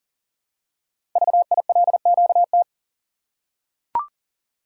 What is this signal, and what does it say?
Morse code, heard as a keyed tone, about 41 wpm, 700 Hz. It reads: VILQT
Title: Code Groups - 5 Letters - 40wpm